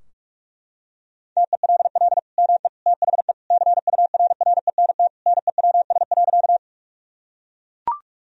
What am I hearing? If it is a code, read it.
Morse code, heard as a keyed tone, about 42 wpm, 700 Hz. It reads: TELL ME THE CURRENT NEWS.
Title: Sentences from Top 1000 Words - Review Part 4 - 45wpm